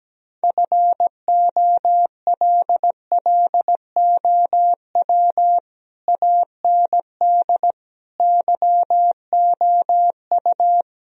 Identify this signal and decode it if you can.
Morse code, heard as a keyed tone, about 17 wpm, 700 Hz. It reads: FOLLOW AND YOU
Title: Sentences from Top 1000 Words - Review Part 1 - 17wpm